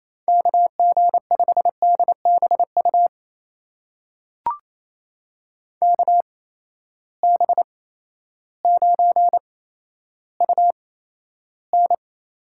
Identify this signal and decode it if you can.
Morse code, heard as a keyed tone, about 28 wpm, spaced as for 13 wpm, 700 Hz. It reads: KG5DBU KB9UN
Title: USA Call Signs - 6 Characters - 8x Character Spacing - 28wpm